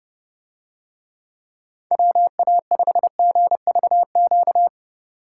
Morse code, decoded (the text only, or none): WA5GVQ